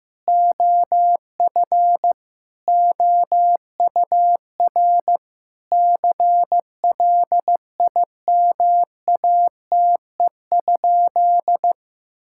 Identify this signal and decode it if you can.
Morse code, heard as a keyed tone, about 15 wpm, 700 Hz. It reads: OF OUR CLIMATE?
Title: Sentences from Top 1600 Words 15wpm